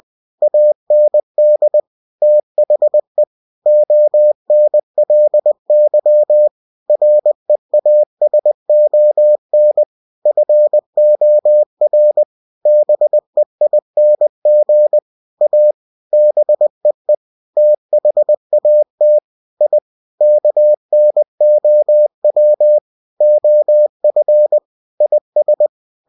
Morse code, heard as a keyed tone, about 20 wpm, 600 Hz. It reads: AND THE ONLY REASON FOR BEING A BEE THAT I KNOW OF IS
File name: cw-books / winnie-the-pooh / winnie-20wpm-page-0002